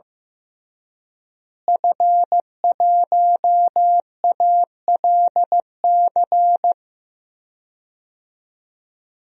Morse code, heard as a keyed tone, about 15 wpm, 700 Hz. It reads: F1ALC
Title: DXCC-100 Call Signs - 5 Characters - Repeated 3x - 15wpm